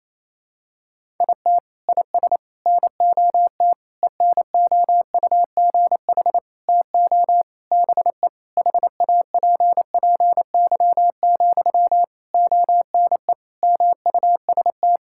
Morse code, heard as a keyed tone, about 28 wpm, 700 Hz. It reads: IT IS NOT ENOUGH TO BE HAPPY, ONE MUST
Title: Sentences from Top 1000 Words - Review Part 6 - 28wpm